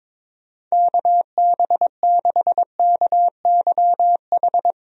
Morse code, heard as a keyed tone, about 22 wpm, 700 Hz. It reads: KB6KYH
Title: USA Call Signs - 6 Characters - Repeated 2x - 22wpm